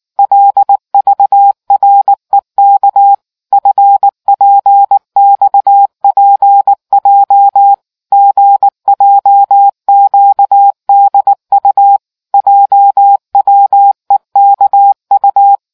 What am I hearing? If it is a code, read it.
Morse code, heard as a keyed tone, about 19 wpm, 800 Hz. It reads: LVREK FPXPJ GJQDU JWEKU